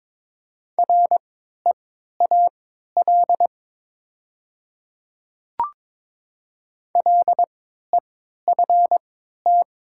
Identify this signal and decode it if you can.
Morse code, heard as a keyed tone, about 22 wpm, spaced as for 12 wpm, 700 Hz. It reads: REAL LEFT
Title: 3x Character Spacing - Top 300 Words - 22wpm